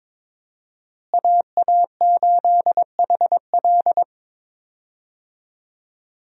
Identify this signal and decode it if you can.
Morse code, heard as a keyed tone, about 22 wpm, 700 Hz. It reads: AA8HL
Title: USA Call Signs - 5 Characters - Repeated 3x - 22wpm